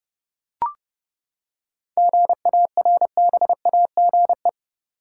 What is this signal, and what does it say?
Morse code, heard as a keyed tone, about 31 wpm, 700 Hz. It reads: GARBAGE
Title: Seven Letter Words - 30wpm